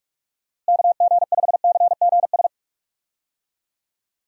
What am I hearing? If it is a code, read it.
Morse code, heard as a keyed tone, about 43 wpm, 700 Hz. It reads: KG5CGS
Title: USA Call Signs - 6 Characters - Repeated 3x - 45wpm